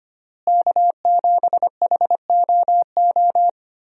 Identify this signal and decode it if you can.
Morse code, heard as a keyed tone, about 25 wpm, 700 Hz. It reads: K7HOO
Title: USA Call Signs - 5 Characters - Repeated 3x - 25wpm